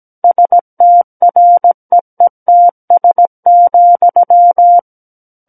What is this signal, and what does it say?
Morse code, heard as a keyed tone, about 17 wpm, 700 Hz. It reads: STREETS,
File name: Book One / 08-Chapter 8 - 17 WPM